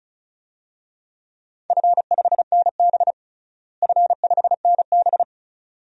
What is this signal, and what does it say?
Morse code, heard as a keyed tone, about 36 wpm, 700 Hz. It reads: F5NB F5NB
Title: DXCC-100 Call Signs - 4 Characters - Repeated 2x - 35wpm